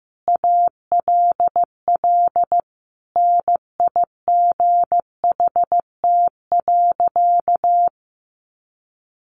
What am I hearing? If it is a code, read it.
Morse code, heard as a keyed tone, about 15 wpm, 700 Hz. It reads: ALL NIGHT.
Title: Sentences from Top 300 Words 15wpm